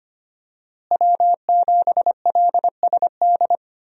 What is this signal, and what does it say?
Morse code, heard as a keyed tone, about 25 wpm, 700 Hz. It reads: W7LSD
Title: USA Call Signs - 5 Characters - Repeated 3x - 25wpm